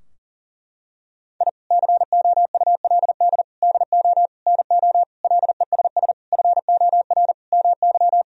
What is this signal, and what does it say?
Morse code, heard as a keyed tone, about 40 wpm, 700 Hz. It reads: I COULD DO NO LESS FOR MY
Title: Sentences from Top 1000 Words - Review Part 2 - 40wpm